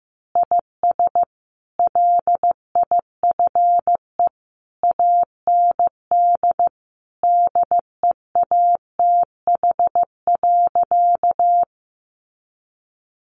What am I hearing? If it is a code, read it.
Morse code, heard as a keyed tone, about 15 wpm, 700 Hz. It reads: IS LIFE AND DEATH.
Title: Sentences from Top 1100 Words 15wpm